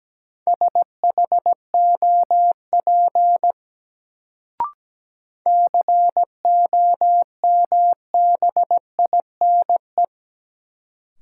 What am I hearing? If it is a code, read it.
Morse code, heard as a keyed tone, about 17 wpm, 700 Hz. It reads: SHOP COMBINE
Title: Sentences from Top 1600 Words - New Word Drill - 17wpm